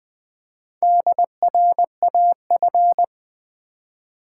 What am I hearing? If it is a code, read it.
Morse code, heard as a keyed tone, about 20 wpm, 700 Hz. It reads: DRAF
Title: Build A Word - 1 Letter At A Time - 5-Letter Words - 20wpm